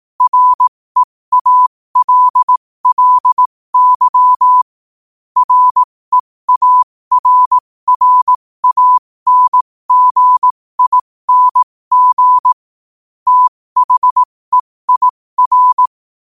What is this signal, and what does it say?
Morse code, heard as a keyed tone, about 18 wpm, spaced as for 15 wpm, 1000 Hz. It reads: REALLY REARRANGING THEIR
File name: qotdfiles / 15wpm